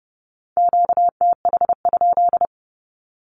Morse code, heard as a keyed tone, about 30 wpm, 700 Hz. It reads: QTH?
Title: Q-Codes with Spoken Meaning - 30wpm